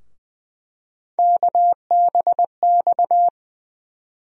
Morse code, heard as a keyed tone, about 20 wpm, 700 Hz. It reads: KBX
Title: USA Call Sign Postfixes - 3 Letters 20wpm